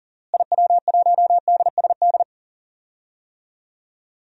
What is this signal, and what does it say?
Morse code, heard as a keyed tone, about 40 wpm, 700 Hz. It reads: IW1DSD
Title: DXCC-100 Call Signs - 6 Characters - Repeated 2x - 40wpm